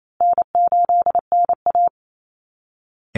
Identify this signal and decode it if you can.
Morse code, heard as a keyed tone, about 28 wpm, 700 Hz. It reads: N8NA
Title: Easy Call Signs Repeated 3x - 28wpm